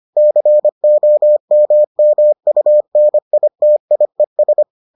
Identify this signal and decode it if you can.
Morse code, heard as a keyed tone, about 25 wpm, 600 Hz. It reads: COMMUNITIES